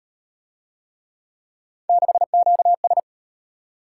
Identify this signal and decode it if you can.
Morse code, heard as a keyed tone, about 32 wpm, 700 Hz. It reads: 6QS